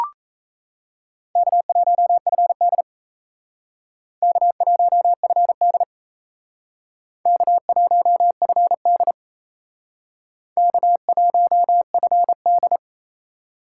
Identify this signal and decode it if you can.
Morse code, heard as a keyed tone, about 36 wpm, 700 Hz. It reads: K1FD K1FD K1FD K1FD